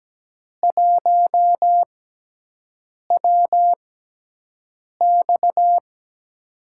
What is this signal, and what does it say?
Morse code, heard as a keyed tone, about 17 wpm, spaced as for 6 wpm, 700 Hz. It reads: 1WX
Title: USA Call Signs - 6x Character Spacing - 17wpm